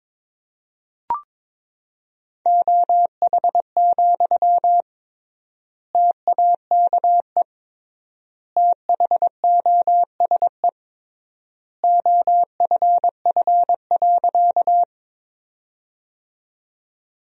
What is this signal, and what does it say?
Morse code, heard as a keyed tone, about 21 wpm, 700 Hz. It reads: OH, TAKE THOSE OFF.
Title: 3x Word-Spacing - Sentences from Top 500 Words - 22wpm